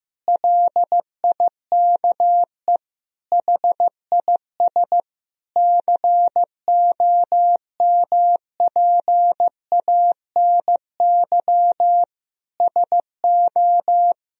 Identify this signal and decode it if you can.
Morse code, heard as a keyed tone, about 16 wpm, 700 Hz. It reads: LIKE HIS COMPANY SO
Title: Sentences from Top 1000 Words - Review Part 3 - 15wpm